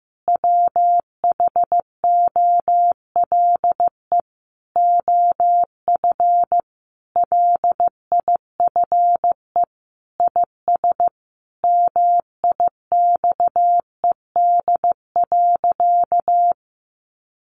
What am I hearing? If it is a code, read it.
Morse code, heard as a keyed tone, about 14 wpm, 700 Hz. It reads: WHOLE OF LIFE IS MIXED.
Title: Sentences from Top 1500 Words 15wpm